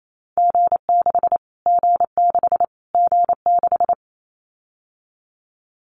Morse code, heard as a keyed tone, about 28 wpm, 700 Hz. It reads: G6 G6 G6
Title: Callsign Prefixes for Top 25 DXCC - Repeated 3x - 28wpm